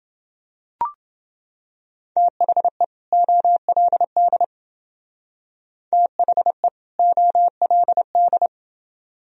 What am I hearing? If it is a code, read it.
Morse code, heard as a keyed tone, about 31 wpm, 700 Hz. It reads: THE OLD THE OLD